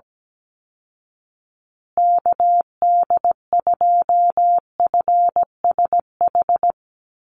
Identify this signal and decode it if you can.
Morse code, heard as a keyed tone, about 17 wpm, 700 Hz. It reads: KD2FSH
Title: Call Signs Repeated 2x - 17wpm